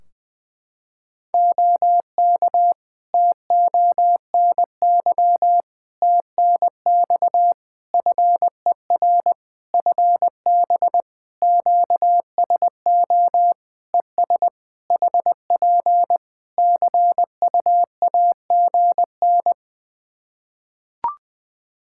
Morse code, heard as a keyed tone, about 20 wpm, 700 Hz. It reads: OK TONY TNX FER FB QSO ES HP CUAGN
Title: Practice QSO - Exchange 4 - Ending - 20wpm